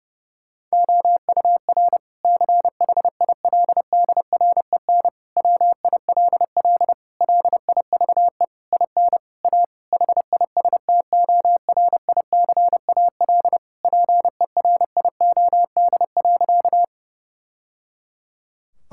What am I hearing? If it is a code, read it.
Morse code, heard as a keyed tone, about 32 wpm, 700 Hz. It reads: OUR CHILDREN WILL LIVE IN A HISTORICAL PERIOD.
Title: Sentences from Top 1000 Words - Review Part 5 - 30wpm